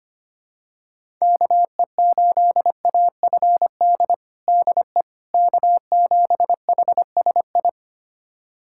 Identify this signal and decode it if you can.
Morse code, heard as a keyed tone, about 25 wpm, 700 Hz. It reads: KE8AFD DE K7HSI
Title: USA Callsign DE Callsign 25wpm